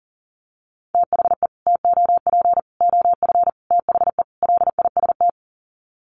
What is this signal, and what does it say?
Morse code, heard as a keyed tone, about 37 wpm, 700 Hz. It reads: THE TOP OF THE LIST
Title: Sets of 5 Words Encore2 40wpm